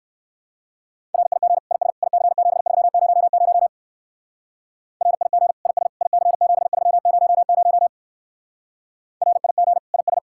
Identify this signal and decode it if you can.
Morse code, heard as a keyed tone, about 56 wpm, 700 Hz. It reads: RIG IS IC7300 RIG IS IC7300 RIG IS